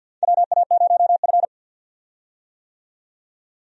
Morse code, heard as a keyed tone, about 48 wpm, 700 Hz. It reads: WA0F
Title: USA Call Signs - 4 Characters - 50wpm